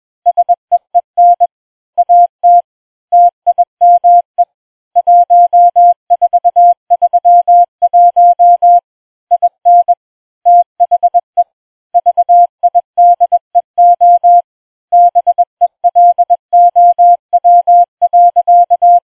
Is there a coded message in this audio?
Yes: Morse code from 0.3 to 19.0 s, heard as a keyed tone, about 20 wpm, 700 Hz. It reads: SEEN AT TIME 1431 IN THE VIDEO BELOW.